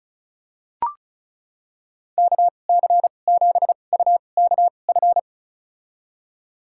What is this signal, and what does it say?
Morse code, heard as a keyed tone, about 34 wpm, spaced as for 25 wpm, 700 Hz. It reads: KC7UKF
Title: USA Call Signs - 6 Characters - 2x Character Spacing - 35wpm